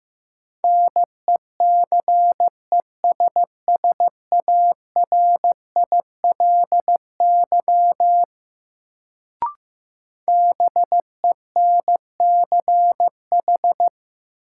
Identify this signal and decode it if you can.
Morse code, heard as a keyed tone, about 15 wpm, 700 Hz. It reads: NECESSARILY BENCH
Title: Sentences from Top 1400 Words - New Word Drill - 15wpm